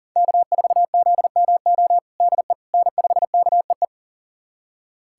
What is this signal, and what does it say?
Morse code, heard as a keyed tone, about 40 wpm, 700 Hz. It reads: K4ZMO DE N5KEE
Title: USA Callsign DE Callsign - 5 Characters - 40wpm